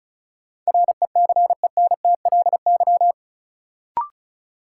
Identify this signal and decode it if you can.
Morse code, heard as a keyed tone, about 36 wpm, 700 Hz. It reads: RECENTLY
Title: Sentences from Top 1200 Words - New Word Drill Alt - 35wpm